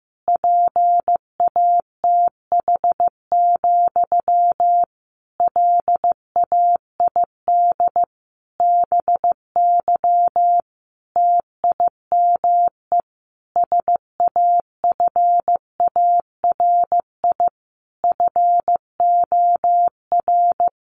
Morse code, heard as a keyed tone, about 15 wpm, 700 Hz. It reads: PATH, LAID BY TIME SAFARI FOR
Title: A Sound Of Thunder 15wpm